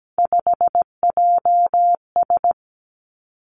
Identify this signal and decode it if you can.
Morse code, heard as a keyed tone, about 17 wpm, 700 Hz. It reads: 5JS
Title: Three Challenging Letter-Number Combinations 17wpm